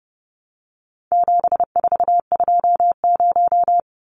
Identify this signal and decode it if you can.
Morse code, heard as a keyed tone, about 30 wpm, 700 Hz. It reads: 7420
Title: Numbers - 4-Digit 30wpm